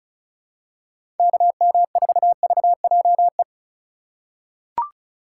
Morse code, heard as a keyed tone, about 34 wpm, 700 Hz. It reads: KM4VJE
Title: USA Call Signs - 6 Characters - 4x Character Spacing - 35wpm